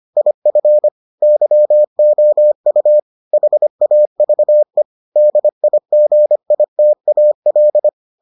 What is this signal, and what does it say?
Morse code, heard as a keyed tone, about 25 wpm, 600 Hz. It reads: IF YOU HAVE DIGITAL